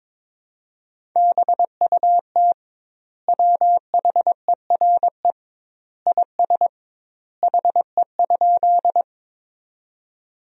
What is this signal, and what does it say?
Morse code, heard as a keyed tone, about 22 wpm, 700 Hz. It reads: BUT WHERE IS HE?
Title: 2x Word-Spacing - Sentences from Top 200 Words - 22wpm